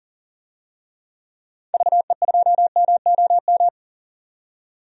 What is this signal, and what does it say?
Morse code, heard as a keyed tone, about 40 wpm, 700 Hz. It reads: VE2MOM
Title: DXCC-100 Call Signs - 6 Characters - 40wpm